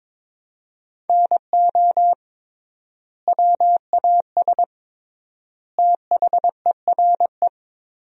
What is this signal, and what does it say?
Morse code, heard as a keyed tone, about 22 wpm, 700 Hz. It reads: NO WAS THERE
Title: 3x Word-Spacing - Sentences from Top 100 Words - 22wpm